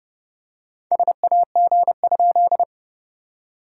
Morse code, heard as a keyed tone, about 30 wpm, 690 Hz. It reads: SAG?
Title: SOTA QSO Practice 30wpm